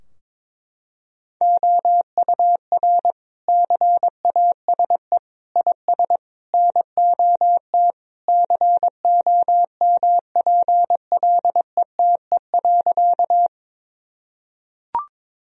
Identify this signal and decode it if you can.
Morse code, heard as a keyed tone, about 21 wpm, 700 Hz. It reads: OUR CASE IS NOT COMPLETE.
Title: Sentences from Top 1000 Words - Review Part 2 - 22wpm